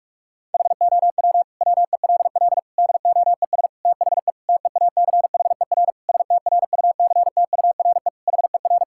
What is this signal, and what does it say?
Morse code, heard as a keyed tone, about 46 wpm, 700 Hz. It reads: HOW WELL DOES THE TEACHER STRUCTURE HER
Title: Sentences from Top 1000 Words - Review Part 6 - 45wpm